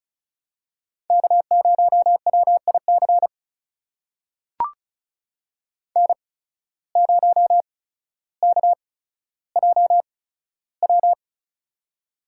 Morse code, heard as a keyed tone, about 34 wpm, spaced as for 15 wpm, 700 Hz. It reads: K0WIC N0KJW
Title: USA Call Signs - 5 Characters - 8x Character Spacing - 35wpm